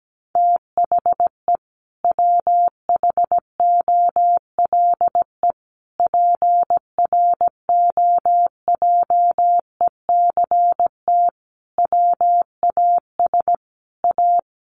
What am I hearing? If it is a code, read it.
Morse code, heard as a keyed tone, about 17 wpm, 700 Hz. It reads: THE WHOLE PROJECT WAS A
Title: Sentences from Top 1400 Words 17wpm